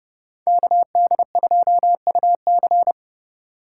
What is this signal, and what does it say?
Morse code, heard as a keyed tone, about 36 wpm, 700 Hz. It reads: KD2UC
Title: Speed Racing - USA Call Signs 30wpm